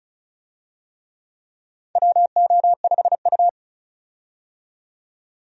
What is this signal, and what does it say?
Morse code, heard as a keyed tone, about 36 wpm, 700 Hz. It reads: WO5U
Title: USA Call Signs - 4 Characters - 35wpm